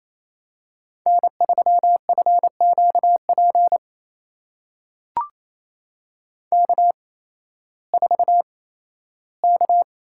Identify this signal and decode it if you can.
Morse code, heard as a keyed tone, about 28 wpm, spaced as for 15 wpm, 700 Hz. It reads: N3FQP K4K
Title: USA Call Signs - 5 Characters - 8x Character Spacing - 28wpm